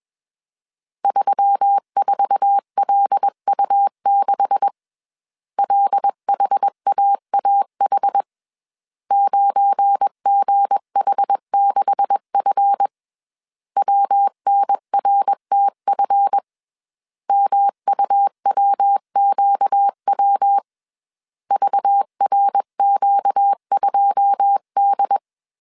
Morse code, heard as a keyed tone, about 21 wpm, 800 Hz. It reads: 34LU6 LHAAH 9GH6F WNRTF MUWQW VRQ2D